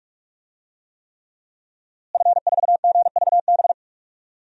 Morse code, heard as a keyed tone, about 46 wpm, 700 Hz. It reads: F4GVB
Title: DXCC-100 Call Signs - 5 Characters - Repeated 3x - 45wpm